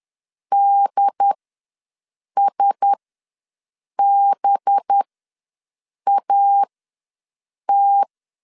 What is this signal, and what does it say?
Morse code, heard as a keyed tone, about 10 wpm, spaced as for 6 wpm, 800 Hz. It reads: DSBAT
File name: LEZIONI / 5 LBVC / 5 LB ws 10wpm 3616